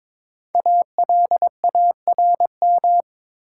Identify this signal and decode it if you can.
Morse code, heard as a keyed tone, about 22 wpm, 700 Hz. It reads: ALARM
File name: 5 Letter Words - First 3 Letters Given - 22wpm